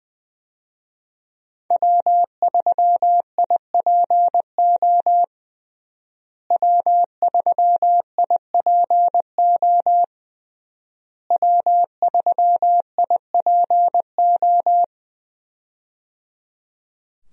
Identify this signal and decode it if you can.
Morse code, heard as a keyed tone, about 20 wpm, 700 Hz. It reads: W3IPO W3IPO W3IPO
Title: USA Call Signs - 5 Characters - Repeated 3x - 20wpm